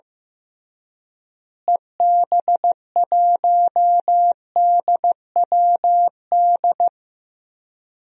Morse code, heard as a keyed tone, about 15 wpm, 700 Hz. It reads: EB1DWD
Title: DXCC-100 Call Signs - 6 Characters - Repeated 3x - 15wpm